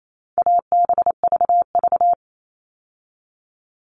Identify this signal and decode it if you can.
Morse code, heard as a keyed tone, about 28 wpm, 700 Hz. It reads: ABVV
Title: All Cut Numbers - 4-Digit 28wpm